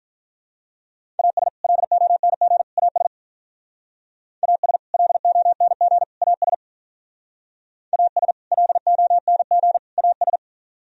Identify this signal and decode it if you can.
Morse code, heard as a keyed tone, about 44 wpm, 700 Hz. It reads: AS LONG AS AS LONG AS AS LONG AS